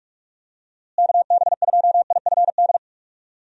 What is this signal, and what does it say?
Morse code, heard as a keyed tone, about 45 wpm, 700 Hz. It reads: KB2IFD
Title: USA Call Signs - 6 Characters - Repeated 2x - 45wpm